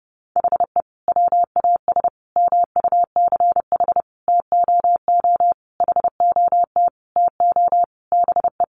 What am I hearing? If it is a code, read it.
Morse code, heard as a keyed tone, about 30 wpm, 700 Hz. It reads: HE WAS MUCH TOO HOT TO BE
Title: Sentences from Top 1200 Words 30wpm